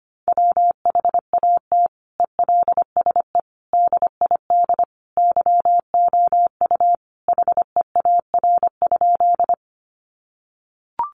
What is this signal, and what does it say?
Morse code, heard as a keyed tone, about 25 wpm, 700 Hz. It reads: WHAT ELSE DID YOU HEAR?
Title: Sentences from Top 800 Words 25wpm